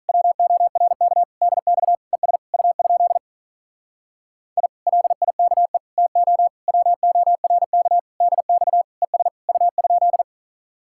Morse code, heard as a keyed tone, about 45 wpm, 700 Hz. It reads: WORK DX ES U? I LIKE TO WORK DX ES U?